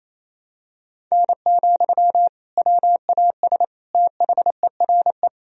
Morse code, heard as a keyed tone, about 28 wpm, 700 Hz. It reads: N, WAS THERE
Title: Sentences from Top 100 Words and Abbreviations 28wpm